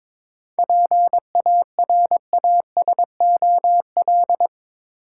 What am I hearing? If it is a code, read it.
Morse code, heard as a keyed tone, about 24 wpm, 700 Hz. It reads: PARASOL
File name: Easy Words - Speed-Racing with Hints - 22wpm